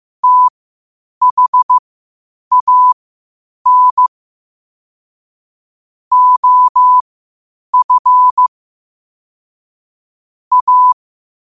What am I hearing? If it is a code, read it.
Morse code, heard as a keyed tone, about 15 wpm, spaced as for 8 wpm, 1000 Hz. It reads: THAN OF A
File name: 8wpm